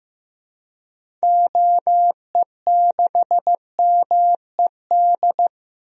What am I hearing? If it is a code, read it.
Morse code, heard as a keyed tone, about 15 wpm, 700 Hz. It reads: OE6MED
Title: DXCC-100 Call Signs - 6 Characters - 15wpm